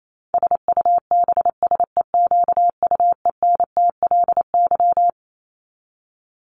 Morse code, heard as a keyed tone, about 28 wpm, 700 Hz. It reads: SUBSEQUENTLY
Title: LY Suffix - Top 100 Words with Base Words 28wpm